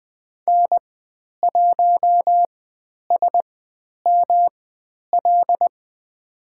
Morse code, heard as a keyed tone, about 20 wpm, spaced as for 10 wpm, 700 Hz. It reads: N1SML
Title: 109 Lesson 18g - Letter m - Call Signs 20wpm